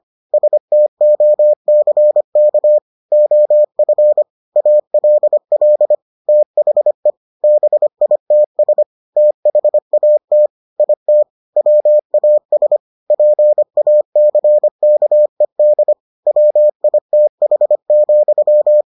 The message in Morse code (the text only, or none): STOCK OF ALL THE BITS THAT IT WAS PACKED WITH,